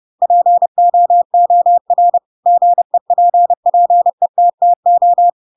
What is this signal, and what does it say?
Morse code, heard as a keyed tone, about 30 wpm, 700 Hz. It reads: POOR GEPPETTO
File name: Pinocchio / Pinocchio ch3 30 30 ik1hge